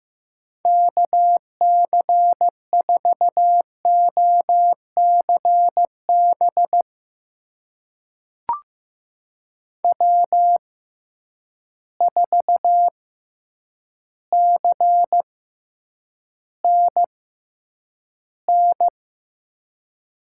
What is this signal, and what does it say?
Morse code, heard as a keyed tone, about 15 wpm, spaced as for 8 wpm, 700 Hz. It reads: KC4OCB W4CNN
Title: USA Call Signs - 6x Character Spacing - 15wpm